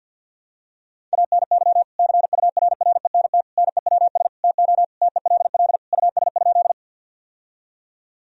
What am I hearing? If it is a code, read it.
Morse code, heard as a keyed tone, about 52 wpm, 700 Hz. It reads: ANY CURRENT NEWS TO TELL US?
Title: Sentences from Top 1000 Words - Review Part 4 - 50wpm